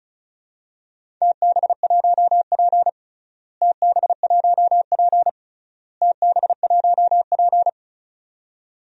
Morse code, heard as a keyed tone, about 34 wpm, 700 Hz. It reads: TB1P TB1P TB1P
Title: DXCC-100 Call Signs - 4 Characters - Repeated 3x - 35wpm